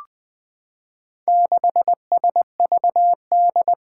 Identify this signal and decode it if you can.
Morse code, heard as a keyed tone, about 20 wpm, 700 Hz. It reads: 6SVD
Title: Four Challenging Letter-Number Combinations 20wpm